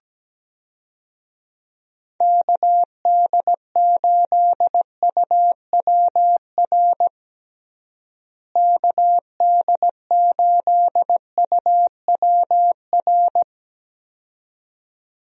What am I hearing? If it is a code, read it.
Morse code, heard as a keyed tone, about 17 wpm, 700 Hz. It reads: KD8UWR KD8UWR
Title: USA Call Signs - 6 Characters - Repeated 2x - 17wpm